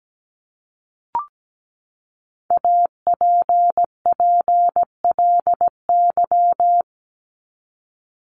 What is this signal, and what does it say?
Morse code, heard as a keyed tone, about 17 wpm, 700 Hz. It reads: APPLY
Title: ED Suffix - Top 100 Words with Base Words 17wpm